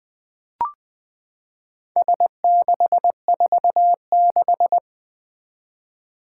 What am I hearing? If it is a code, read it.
Morse code, heard as a keyed tone, about 20 wpm, 700 Hz. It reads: S646
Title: Four Challenging Letter-Number Combinations 20wpm